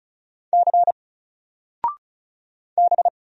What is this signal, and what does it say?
Morse code, heard as a keyed tone, about 33 wpm, 700 Hz. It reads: C B
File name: Single-Letter-Number-35wpm